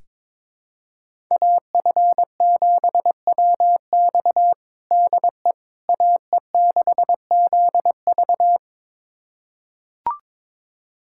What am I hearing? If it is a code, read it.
Morse code, heard as a keyed tone, about 22 wpm, 700 Hz. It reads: AF7WX DE AE6ZV
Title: USA Callsign DE Callsign - 5 Characters - 22wpm